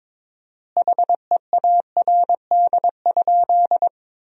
Morse code, heard as a keyed tone, about 22 wpm, 700 Hz. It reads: HEARD?
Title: 4x Word-Spacing - Sentences from Top 400 Words - 22wpm